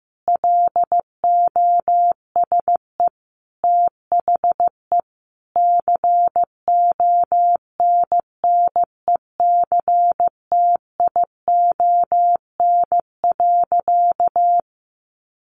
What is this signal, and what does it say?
Morse code, heard as a keyed tone, about 15 wpm, 700 Hz. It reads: LOSE THE CONNECTION.
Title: Sentences from Top 1300 Words 15wpm